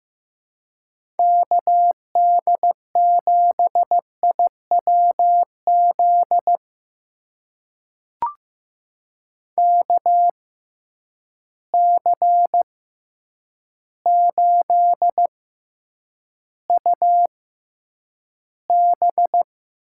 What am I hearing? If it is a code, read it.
Morse code, heard as a keyed tone, about 16 wpm, spaced as for 8 wpm, 700 Hz. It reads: KD7IWZ KC8UB
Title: USA Call Signs - 6 Characters - 6x Character Spacing - 15wpm